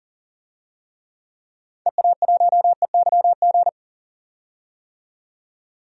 Morse code, heard as a keyed tone, about 38 wpm, 700 Hz. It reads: EA1EYG
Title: DXCC-100 Call Signs - 6 Characters - 40wpm